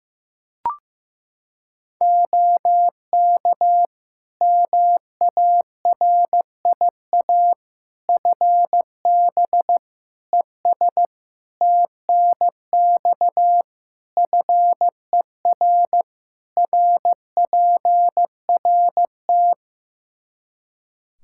Morse code, heard as a keyed tone, about 15 wpm, 700 Hz. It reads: OK MARIA FB ES TNX FER RPRT
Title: Practice QSO - Exchange 2 - Info - 15wpm